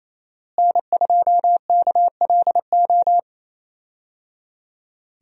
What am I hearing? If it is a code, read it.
Morse code, heard as a keyed tone, about 27 wpm, 700 Hz. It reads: N2KLO
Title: USA Call Signs - 5 Characters - Repeated 3x - 28wpm